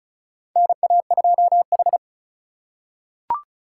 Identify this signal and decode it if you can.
Morse code, heard as a keyed tone, about 36 wpm, 700 Hz. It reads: NA2H
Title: USA Call Signs - 4 Characters - 3x Character Spacing - 35wpm